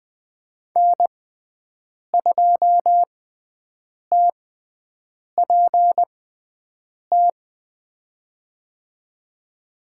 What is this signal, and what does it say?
Morse code, heard as a keyed tone, about 20 wpm, spaced as for 7 wpm, 700 Hz. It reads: N2TPT
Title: USA Call Signs - 5 Characters - 6x Character Spacing - 20wpm